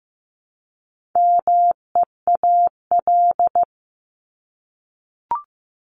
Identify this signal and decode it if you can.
Morse code, heard as a keyed tone, about 15 wpm, 700 Hz. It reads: MEAL
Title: Top 1751-2000 Words 15wpm